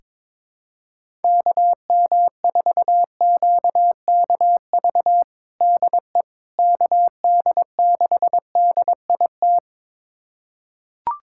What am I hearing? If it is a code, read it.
Morse code, heard as a keyed tone, about 22 wpm, 700 Hz. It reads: KM4QKV DE KD6DIT
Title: USA Callsign DE Callsign - 6 Characters - 22wpm